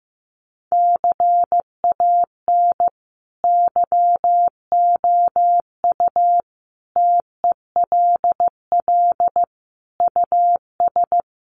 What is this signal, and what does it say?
Morse code, heard as a keyed tone, about 15 wpm, 700 Hz. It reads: CAN YOU TELL US
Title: Sets of 5 Words Encore2 15wpm